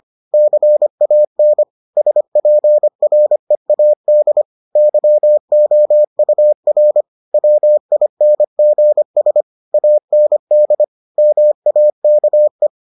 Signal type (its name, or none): Morse code